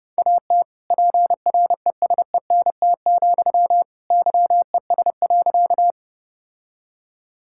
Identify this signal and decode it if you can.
Morse code, heard as a keyed tone, about 32 wpm, 700 Hz. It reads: AT PRESENT, YES.
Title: Sentences from Top 1000 Words - Review Part 5 - 30wpm